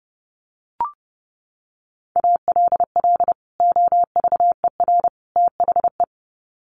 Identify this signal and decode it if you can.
Morse code, heard as a keyed tone, about 30 wpm, 700 Hz. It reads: ALL OVER THE
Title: Sets of 3 Words 30wpm